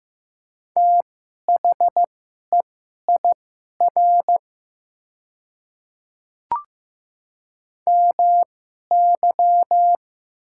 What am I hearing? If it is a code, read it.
Morse code, heard as a keyed tone, about 15 wpm, spaced as for 11 wpm, 700 Hz. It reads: THEIR MY
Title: 2x Character Spacing - Top 100 Words - 15wpm